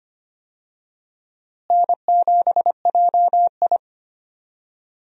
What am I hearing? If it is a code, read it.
Morse code, heard as a keyed tone, about 25 wpm, 700 Hz. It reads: N7JI
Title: USA Call Signs - 4 Characters - Repeated 3x - 25wpm